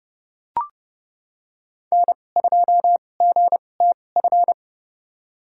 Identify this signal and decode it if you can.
Morse code, heard as a keyed tone, about 30 wpm, spaced as for 22 wpm, 700 Hz. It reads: N2GTF
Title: USA Call Signs - 5 Characters - 2x Character Spacing - 30wpm